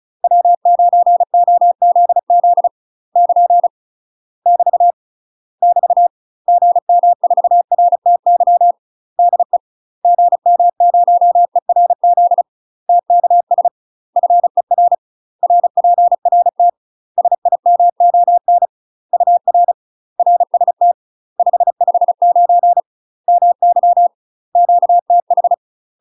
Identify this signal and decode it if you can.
Morse code, heard as a keyed tone, about 34 wpm, 700 Hz. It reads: W9OZZ <KN> <BT> <BT> GM4RTY DE GM0ERZ TKS FER RPRT SIMON UR RST 559 MY QTH
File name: UK MASTER QSOs TRAINING / 35WPM / UK MASTER QSOs 0040